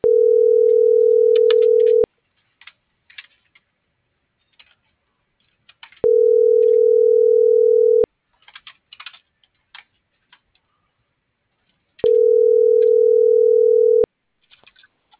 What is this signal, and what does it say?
Ringback tone (North America), 440 + 480 Hz.
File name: CENTRAL MABE / out xquenda.trinidad - .68971